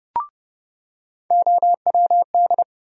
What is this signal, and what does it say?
Morse code, heard as a keyed tone, about 29 wpm, 700 Hz. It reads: OWD
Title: USA Call Sign Postfixes - 3 Letters 30wpm